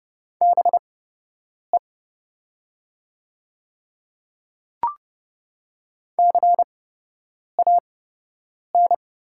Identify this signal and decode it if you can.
Morse code, heard as a keyed tone, about 29 wpm, spaced as for 8 wpm, 700 Hz. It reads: BE CAN
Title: 8x Character Spacing - Top 100 Words - 30wpm